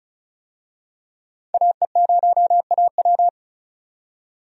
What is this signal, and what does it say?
Morse code, heard as a keyed tone, about 36 wpm, 700 Hz. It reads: AE0AW
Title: USA Call Signs - 5 Characters - 35wpm